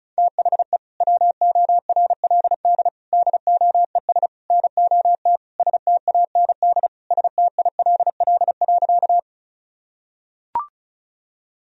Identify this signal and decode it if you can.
Morse code, heard as a keyed tone, about 34 wpm, 700 Hz. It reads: THE WORLD DOES NOT STAND STILL.
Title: Sentences from Top 1000 Words - Review Part 1 - 35wpm